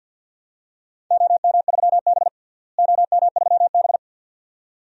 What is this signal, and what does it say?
Morse code, heard as a keyed tone, about 46 wpm, 700 Hz. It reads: OM3B OM3B
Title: DXCC-100 Call Signs - 4 Characters - Repeated 2x - 50wpm